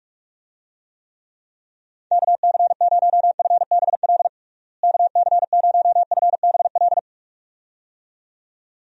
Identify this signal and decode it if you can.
Morse code, heard as a keyed tone, about 44 wpm, 700 Hz. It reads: KC0FBL KC0FBL
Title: USA Call Signs - 6 Characters - Repeated 2x - 45wpm